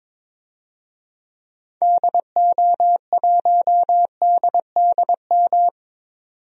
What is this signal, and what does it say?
Morse code, heard as a keyed tone, about 22 wpm, 700 Hz. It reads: DO1DDM
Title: DXCC-100 Call Signs - 6 Characters - 22wpm